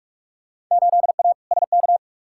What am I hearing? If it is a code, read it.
Morse code, heard as a keyed tone, about 46 wpm, 700 Hz. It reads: 8A SK
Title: Field Day Exchanges Encore - 45wpm